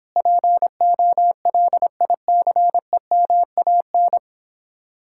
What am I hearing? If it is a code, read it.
Morse code, heard as a keyed tone, about 31 wpm, 700 Hz. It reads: POLICEMAN